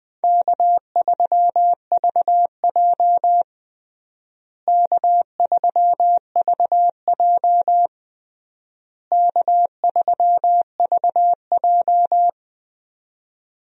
Morse code, heard as a keyed tone, about 20 wpm, 700 Hz. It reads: K3VJ K3VJ K3VJ
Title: USA Call Signs - 4 Characters - Repeated 3x - 20wpm